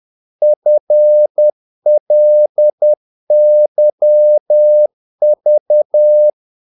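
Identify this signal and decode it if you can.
Morse code, heard as a keyed tone, about 10 wpm, 600 Hz. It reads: FLYV
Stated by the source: Morse practice of short Danish words at 10 wpm, from CW med Gnister.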